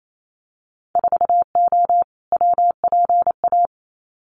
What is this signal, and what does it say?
Morse code, heard as a keyed tone, about 28 wpm, 700 Hz. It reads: 4O WPA
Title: Winter Field Day Exchanges 28wpm